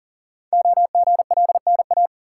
Morse code, heard as a keyed tone, about 42 wpm, 690 Hz. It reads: OGLNA
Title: Code Groups - 5 Letters - 40wpm